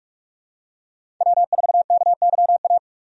Morse code, heard as a keyed tone, about 46 wpm, 700 Hz. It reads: W4KYA
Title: USA Call Signs - 5 Characters - 45wpm